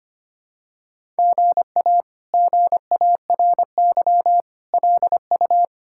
Morse code, heard as a keyed tone, about 25 wpm, 700 Hz. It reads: GA GARY LU
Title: K1USN SST QSO Practice - 25wpm